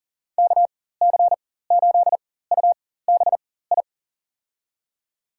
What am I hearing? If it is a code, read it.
Morse code, heard as a keyed tone, about 40 wpm, spaced as for 19 wpm, 700 Hz. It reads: KC8UBI
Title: USA Call Signs - 6 Characters - 4x Character Spacing - 40wpm